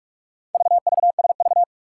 Morse code, heard as a keyed tone, about 44 wpm, 700 Hz. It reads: VVSV
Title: Four Challenging Letter-Number Combinations 45wpm